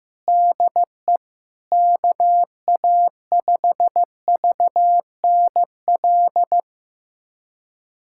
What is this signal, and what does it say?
Morse code, heard as a keyed tone, about 15 wpm, 700 Hz. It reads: DE KA5VNL
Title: USA Callsign DE Callsign - 6 Characters - 15wpm